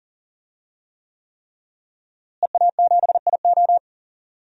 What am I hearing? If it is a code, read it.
Morse code, heard as a keyed tone, about 40 wpm, 700 Hz. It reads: EA7IO
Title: DXCC-100 Call Signs - 5 Characters - Repeated 3x - 40wpm